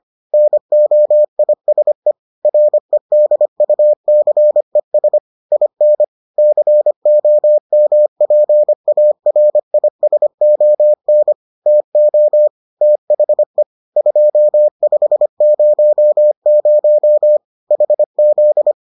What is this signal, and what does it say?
Morse code, heard as a keyed tone, about 25 wpm, 600 Hz. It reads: NOISE REDUCES IN COMPARISON TO THE 2500 HZ